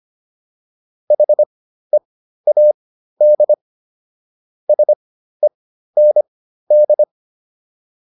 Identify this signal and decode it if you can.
Morse code, heard as a keyed tone, about 25 wpm, spaced as for 13 wpm, 600 Hz. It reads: HEAD SEND